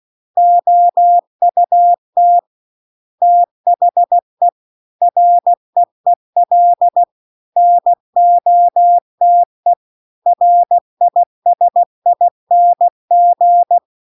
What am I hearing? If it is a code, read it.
Morse code, heard as a keyed tone, about 16 wpm, 700 Hz. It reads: OUT THE REEL NOTE RISING